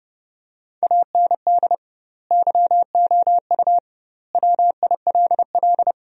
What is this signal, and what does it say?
Morse code, heard as a keyed tone, about 30 wpm, 700 Hz. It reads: AND YOU WILL
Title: 2x Word-Spacing - Sentences from Top 400 Words - 30wpm